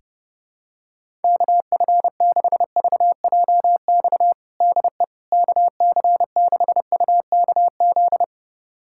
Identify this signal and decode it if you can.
Morse code, heard as a keyed tone, about 32 wpm, 700 Hz. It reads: KF6VJX DE KC6UKZ
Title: USA Callsign DE Callsign - 6 Characters - 30wpm